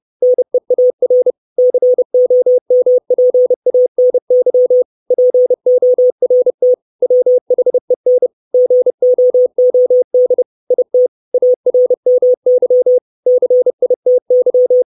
Morse code, heard as a keyed tone, about 30 wpm, 500 Hz. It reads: NEAR COMPANY PORT WHEN GOOD IT ARMY CITY
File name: 632 Words 30WPM